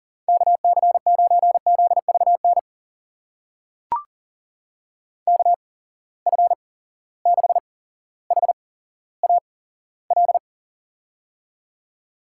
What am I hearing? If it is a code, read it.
Morse code, heard as a keyed tone, about 40 wpm, spaced as for 17 wpm, 700 Hz. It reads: KC9ZVN KF6HAL
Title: USA Call Signs - 6 Characters - 8x Character Spacing - 40wpm